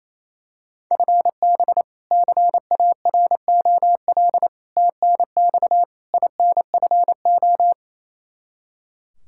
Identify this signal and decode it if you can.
Morse code, heard as a keyed tone, about 28 wpm, 700 Hz. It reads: FB CAROL TNX INFO
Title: K1USN SST QSO Practice 28wpm